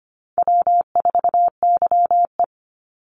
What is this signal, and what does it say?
Morse code, heard as a keyed tone, about 25 wpm, 700 Hz. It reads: W4YE
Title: Easy Call Signs Repeated 2x - 25wpm